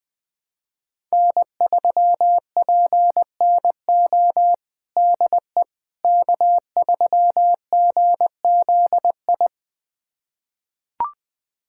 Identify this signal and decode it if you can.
Morse code, heard as a keyed tone, about 20 wpm, 700 Hz. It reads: N3PNO DE K3GZI
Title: USA Callsign DE Callsign - 5 Characters - 20wpm